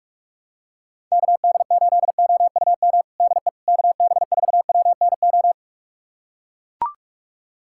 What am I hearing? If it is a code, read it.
Morse code, heard as a keyed tone, about 43 wpm, 700 Hz. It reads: KD8OUM DE KB4WNO
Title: USA Callsign DE Callsign 45wpm